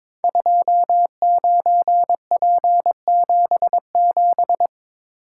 Morse code, heard as a keyed tone, about 22 wpm, 700 Hz. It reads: 29P77
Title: Code Groups - 5 Letter:Numbers - 22wpm